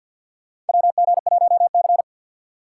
Morse code, heard as a keyed tone, about 53 wpm, 700 Hz. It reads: WG1C
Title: USA Call Signs - 4 Characters - 2x Character Spacing - 50wpm